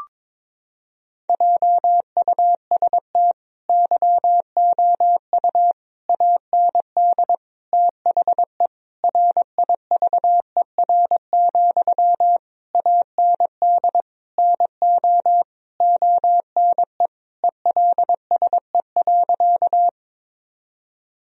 Morse code, heard as a keyed tone, about 22 wpm, 700 Hz. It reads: JUST YOU AND THE RIVER, AND NO ONE ELSE.
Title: Sentences from Top 1000 Words - Review Part 1 - 22wpm